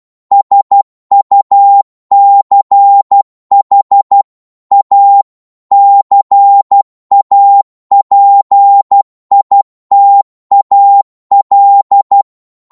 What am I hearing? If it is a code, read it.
Morse code, heard as a keyed tone, about 12 wpm, 800 Hz. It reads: SUCH A CAPITAL
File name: Morse / psg-12wpm